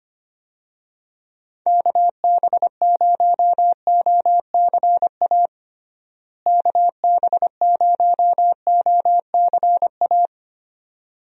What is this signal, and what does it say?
Morse code, heard as a keyed tone, about 25 wpm, 700 Hz. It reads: KB0OCA KB0OCA
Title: USA Call Signs - 6 Characters - Repeated 2x - 25wpm